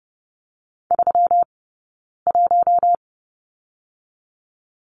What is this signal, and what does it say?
Morse code, heard as a keyed tone, about 30 wpm, 700 Hz. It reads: 3 1
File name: Lesson 20d - Number 3 - Two Characters 30wpm